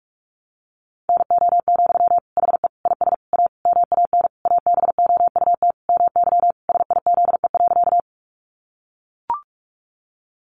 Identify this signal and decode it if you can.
Morse code, heard as a keyed tone, about 44 wpm, 700 Hz. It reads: NO, HE IS A MAN ABOUT MY SIZE.
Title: Sentences from Top 400 Words 45wpm